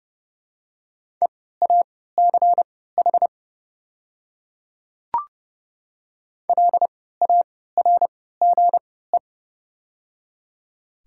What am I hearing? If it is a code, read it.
Morse code, heard as a keyed tone, about 30 wpm, spaced as for 17 wpm, 700 Hz. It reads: EACH LARGE
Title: 3x Character Spacing - Top 200 Words - 30wpm